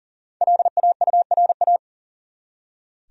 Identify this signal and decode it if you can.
Morse code, heard as a keyed tone, about 40 wpm, 700 Hz. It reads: LAURA
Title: USA Common Names - Max 5 Letters - 40wpm